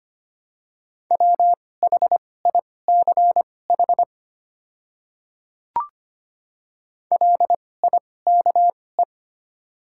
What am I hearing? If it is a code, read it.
Morse code, heard as a keyed tone, about 25 wpm, spaced as for 18 wpm, 700 Hz. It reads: WHICH LIKE
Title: 2x Character Spacing - Top 100 Words - 25wpm